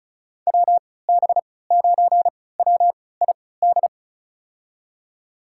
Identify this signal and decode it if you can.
Morse code, heard as a keyed tone, about 36 wpm, spaced as for 20 wpm, 700 Hz. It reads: WB9WID
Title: USA Call Signs - 6 Characters - 3x Character Spacing - 35wpm